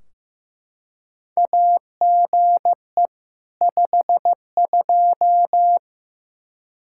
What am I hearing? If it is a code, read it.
Morse code, heard as a keyed tone, about 15 wpm, 700 Hz. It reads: AGE 52
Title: Practice QSO - Exchange 3 - More Info - 15wpm